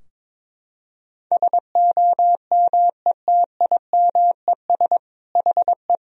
Morse code, heard as a keyed tone, about 22 wpm, 700 Hz. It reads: SOMETIMES HE
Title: Sentences from Top 1000 Words - Review Part 4 - 22wpm